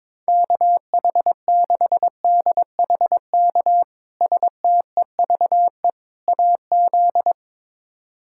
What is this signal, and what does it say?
Morse code, heard as a keyed tone, about 22 wpm, 700 Hz. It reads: KH6DHK STEVE AZ
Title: K1USN SST QSO Practice - 22wpm